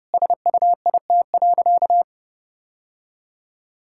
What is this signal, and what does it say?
Morse code, heard as a keyed tone, about 30 wpm, 700 Hz. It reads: SUIT.
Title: Sentences from Top 1600 Words 30wpm